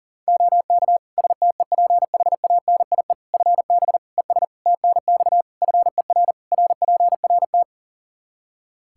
Morse code, heard as a keyed tone, about 40 wpm, 700 Hz. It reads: OK STEPHANIE FB ES TNX FER RPRT
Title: Practice QSO - Exchange 2 - Info - 40wpm